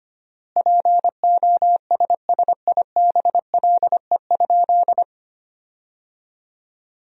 Morse code, heard as a keyed tone, about 25 wpm, 700 Hz. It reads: POSSIBLE?
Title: 4x Word-Spacing - Sentences from Top 500 Words - 25wpm